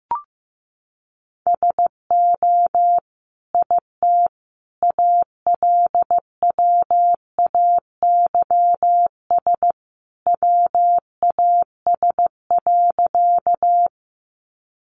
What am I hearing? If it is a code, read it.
Morse code, heard as a keyed tone, about 15 wpm, 700 Hz. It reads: SO IT ALWAYS WAS.
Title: Sentences from Top 300 Words 15wpm